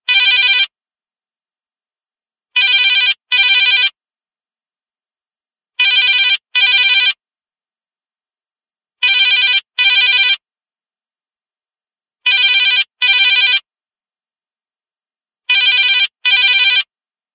There is no code.